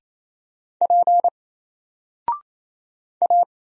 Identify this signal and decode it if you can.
Morse code, heard as a keyed tone, about 27 wpm, 700 Hz. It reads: P A
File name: Single-Letter-Number-28wpm